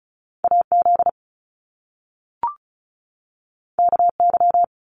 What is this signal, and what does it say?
Morse code, heard as a keyed tone, about 36 wpm, 700 Hz. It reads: AZ KY
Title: US States 35wpm